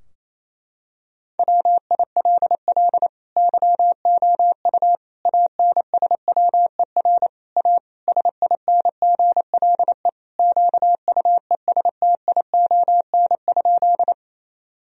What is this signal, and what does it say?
Morse code, heard as a keyed tone, about 28 wpm, 700 Hz. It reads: WILL YOU ANSWER A SINGLE QUESTION?
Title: Sentences from Top 1000 Words - Review Part 4 - 28wpm